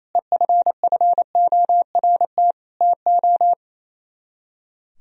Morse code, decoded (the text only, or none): EFFORT TO